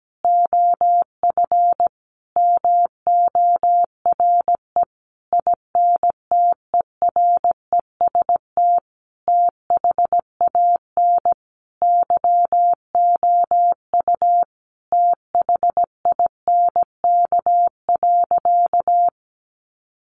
Morse code, heard as a keyed tone, about 17 wpm, 700 Hz. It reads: OF MORE INTEREST THAN YOU THINK.
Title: Sentences from Top 400 Words 17wpm